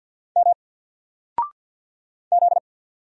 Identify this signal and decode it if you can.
Morse code, heard as a keyed tone, about 50 wpm, 700 Hz. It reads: M Z
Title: Single Letter-Number 50wpm